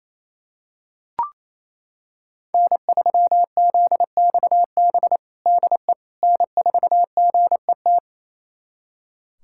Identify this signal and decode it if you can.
Morse code, heard as a keyed tone, about 28 wpm, 700 Hz. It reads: N3ZXB DE N4GET
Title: USA Callsign DE Callsign - 5 Characters - 28wpm